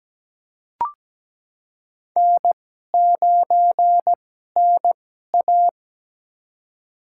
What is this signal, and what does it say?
Morse code, heard as a keyed tone, about 17 wpm, spaced as for 12 wpm, 700 Hz. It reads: N9NA
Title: USA Call Signs - 4 Characters - 2x Character Spacing - 17wpm